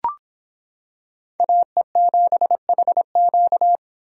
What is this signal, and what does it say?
Morse code, heard as a keyed tone, about 25 wpm, 700 Hz. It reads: AE7HQ